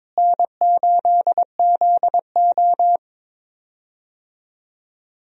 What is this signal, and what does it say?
Morse code, heard as a keyed tone, about 22 wpm, 700 Hz. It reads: N8ZO
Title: USA Call Signs - 4 Characters - Repeated 2x - 22wpm